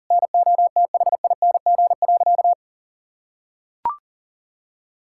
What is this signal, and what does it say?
Morse code, heard as a keyed tone, about 40 wpm, 700 Hz. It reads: NOTHING.
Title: Sentences from Top 1000 Words - Review Part 6 - 40wpm